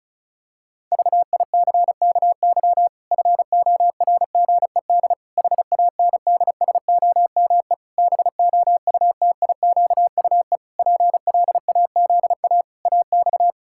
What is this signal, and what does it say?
Morse code, heard as a keyed tone, about 34 wpm, 700 Hz. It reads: VICKY FORGED HANDSOME BOUTIQUE PLAZA AX
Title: Pangram Challenge v2 - 35wpm